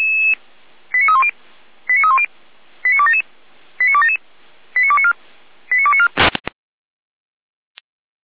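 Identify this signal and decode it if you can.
Five-tone selective call (ZVEI), heard as quick runs of five tones, tones of about 70 ms. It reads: 89310 89310 89277 89277 89294 89294
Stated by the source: Five-tone ZVEI selective call received by SDR.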